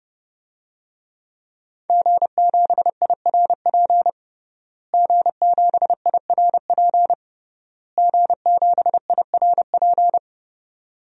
Morse code, heard as a keyed tone, about 30 wpm, 700 Hz. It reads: G7IRP G7IRP G7IRP
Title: DXCC-100 Call Signs - 5 Characters - Repeated 3x - 30wpm